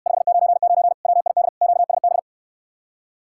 Morse code, heard as a keyed tone, about 69 wpm, 700 Hz. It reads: 599 WID QSB